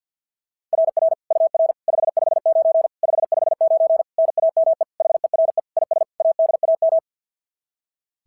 Morse code, heard as a keyed tone, about 50 wpm, 690 Hz. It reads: RR UR 559 559 NAME HERE IS ADAM
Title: SOTA QSO Practice 50wpm